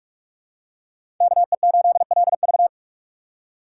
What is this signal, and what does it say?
Morse code, heard as a keyed tone, about 46 wpm, 700 Hz. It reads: KE8LV
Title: USA Call Signs - 5 Characters - 4x Character Spacing - 45wpm